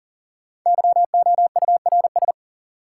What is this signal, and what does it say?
Morse code, heard as a keyed tone, about 40 wpm, 700 Hz. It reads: YOURS
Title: Sentences from Top 1500 Words - New Word Drill - 40wpm